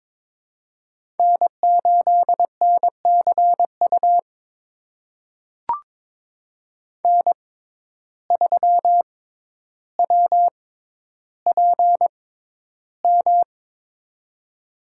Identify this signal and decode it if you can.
Morse code, heard as a keyed tone, about 22 wpm, spaced as for 11 wpm, 700 Hz. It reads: N8NCU N3WPM
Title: USA Call Signs - 5 Characters - 6x Character Spacing - 22wpm